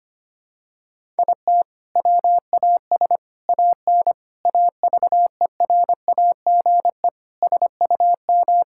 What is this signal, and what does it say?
Morse code, heard as a keyed tone, about 25 wpm, 700 Hz. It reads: IT WAS AN AVERAGE SUM
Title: Sentences from Top 1000 Words - Review Part 6 - 25wpm